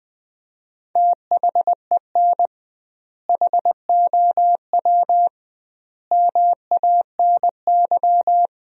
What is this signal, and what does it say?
Morse code, heard as a keyed tone, about 20 wpm, 700 Hz. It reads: THEN HOW MANY
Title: 2x Word-Spacing - Sentences from Top 100 Words - 20wpm